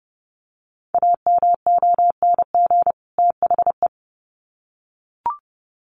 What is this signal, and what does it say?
Morse code, heard as a keyed tone, about 30 wpm, 700 Hz. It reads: AMONG THE
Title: Sets of 2 Words 30wpm